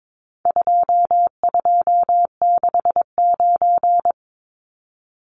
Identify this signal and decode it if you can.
Morse code, heard as a keyed tone, about 22 wpm, 700 Hz. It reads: 2269
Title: Numbers - 4-Digit 22wpm